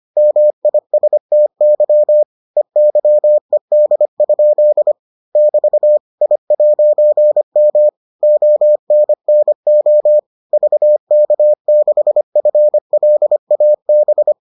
Morse code, heard as a keyed tone, about 25 wpm, 600 Hz. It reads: MISTY EYED? <BT> I'M ONNO VK6FLAB